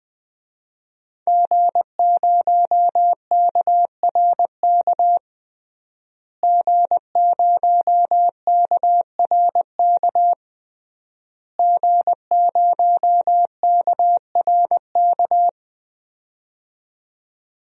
Morse code, heard as a keyed tone, about 20 wpm, 700 Hz. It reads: G0KRK G0KRK G0KRK
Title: DXCC-100 Call Signs - 5 Characters - Repeated 3x - 20wpm